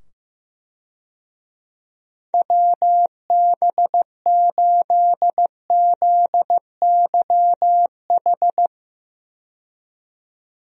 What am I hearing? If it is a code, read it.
Morse code, heard as a keyed tone, about 15 wpm, 700 Hz. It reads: WB8ZYH
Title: USA Call Signs - 6 Characters - 15wpm